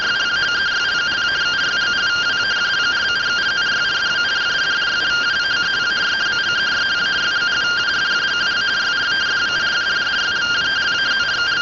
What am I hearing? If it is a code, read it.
Two-tone teleprinter signal, about 170 Hz shift.
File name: findings / CT 205 ax mips, iDose 3 0004.gen-25 rtty rx